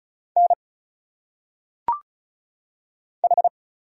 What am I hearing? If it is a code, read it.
Morse code, heard as a keyed tone, about 34 wpm, 700 Hz. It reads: N H
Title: Single Letters - Phonetic Pronunciation 35wpm